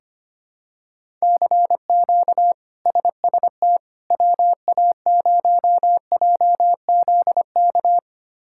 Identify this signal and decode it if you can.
Morse code, heard as a keyed tone, about 25 wpm, 700 Hz. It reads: CQ SST WA0JZK
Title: K1USN SST QSO Practice - 25wpm